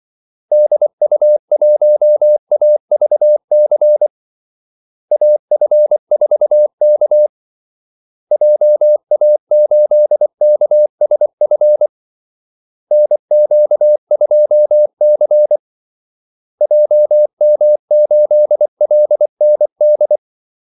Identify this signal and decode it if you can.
Morse code, heard as a keyed tone, about 24 wpm, 600 Hz. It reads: DU1AVC AF4K JA8KSF NQ2C JM8LND
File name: Callsigns-24wpm